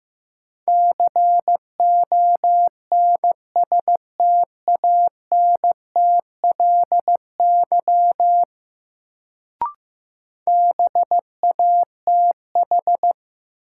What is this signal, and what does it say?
Morse code, heard as a keyed tone, about 15 wpm, 700 Hz. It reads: CONSTANTLY BATH
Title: Sentences from Top 1600 Words - New Word Drill Alt - 15wpm